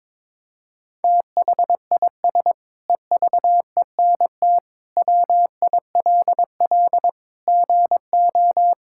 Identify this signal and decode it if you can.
Morse code, heard as a keyed tone, about 22 wpm, 700 Hz. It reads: THIS EVENT WILL GO
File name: Sentences with 9-Word Phrases - 22wpm